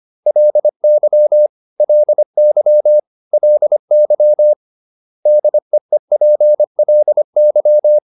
Morse code, heard as a keyed tone, about 25 wpm, 600 Hz. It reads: LY LY LY DEEPLY